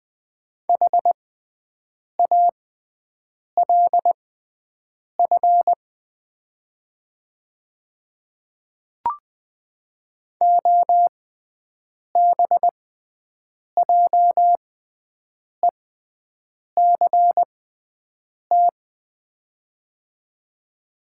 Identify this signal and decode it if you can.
Morse code, heard as a keyed tone, about 20 wpm, spaced as for 7 wpm, 700 Hz. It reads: HALF OBJECT
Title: 6x Character Spacing - Top 500 Words - 20wpm